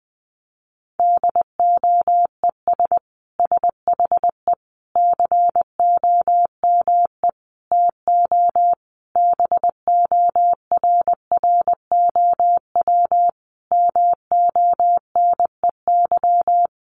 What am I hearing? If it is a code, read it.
Morse code, heard as a keyed tone, about 20 wpm, 700 Hz. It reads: DOES SHE COME TO BORROW MONEY
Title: Sentences from Top 1500 Words 20wpm